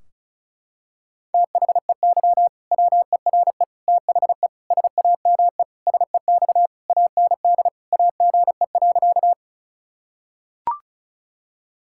Morse code, heard as a keyed tone, about 36 wpm, 700 Hz. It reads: THEY WERE THE SAME SEX AND AGE.
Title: Sentences from Top 1600 Words 35wpm